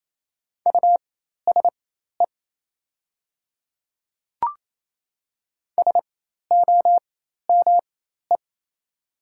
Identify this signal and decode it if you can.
Morse code, heard as a keyed tone, about 28 wpm, spaced as for 13 wpm, 700 Hz. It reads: USE SOME
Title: 4x Character Spacing - Top 200 Words - 28wpm